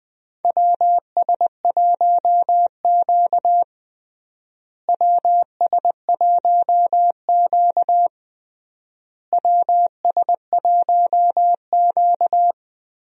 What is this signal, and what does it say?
Morse code, heard as a keyed tone, about 20 wpm, 700 Hz. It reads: WS1Q WS1Q WS1Q
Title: USA Call Signs - 4 Characters - Repeated 3x - 20wpm